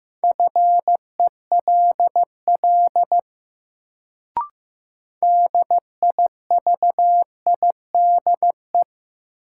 Morse code, heard as a keyed tone, about 15 wpm, 700 Hz. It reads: FELL DIVIDE
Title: Sentences from Top 1600 Words - New Word Drill Alt - 15wpm